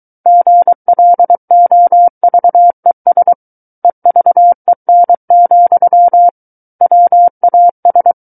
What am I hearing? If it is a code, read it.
Morse code, heard as a keyed tone, about 23 wpm, 700 Hz. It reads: GLOVES EVEN, WAS
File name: Book One / 14-Chapter 14 - 23 WPM